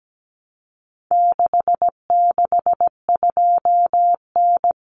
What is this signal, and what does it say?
Morse code, heard as a keyed tone, about 17 wpm, 700 Hz. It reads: 662N
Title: Common Cut Numbers - 4-Digit 17wpm